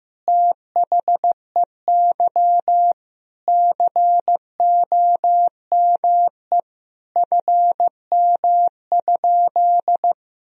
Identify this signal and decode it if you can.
Morse code, heard as a keyed tone, about 15 wpm, 700 Hz. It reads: THEY COME FM?
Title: Sentences from Top 200 Words and Abbreviations 15wpm